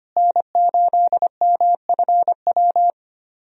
Morse code, heard as a keyed tone, about 25 wpm, 700 Hz. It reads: N8MFW
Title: USA Call Signs - 5 Characters - Repeated 3x - 25wpm